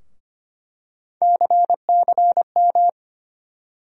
Morse code, heard as a keyed tone, about 25 wpm, 700 Hz. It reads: CCM
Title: USA Call Sign Postfixes - 3 Letters 25wpm